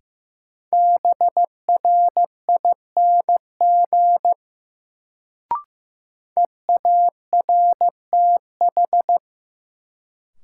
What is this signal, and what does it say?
Morse code, heard as a keyed tone, about 15 wpm, 700 Hz. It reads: BRING EARTH
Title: Sets of 5 Words Encore - New Word Drill - 15wpm